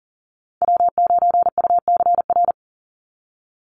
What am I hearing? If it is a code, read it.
Morse code, heard as a keyed tone, about 40 wpm, 700 Hz. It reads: W9UCR
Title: Call Signs - 40wpm